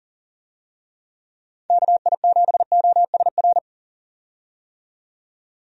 Morse code, heard as a keyed tone, about 40 wpm, 700 Hz. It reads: KI7OSR
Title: USA Call Signs - 6 Characters - 40wpm